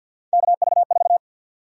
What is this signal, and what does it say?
Morse code, heard as a keyed tone, about 53 wpm, 700 Hz. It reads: KV4
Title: USA Call Sign Prefixes - 50wpm